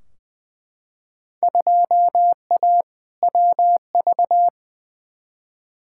Morse code, heard as a keyed tone, about 20 wpm, 700 Hz. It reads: 2A WV
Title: Field Day Exchanges Encore - 20wpm